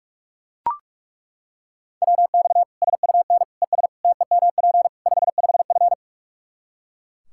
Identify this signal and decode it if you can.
Morse code, heard as a keyed tone, about 41 wpm, 700 Hz. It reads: WX SUN ES TEMP 55F
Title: K1USN SST QSO Practice 45wpm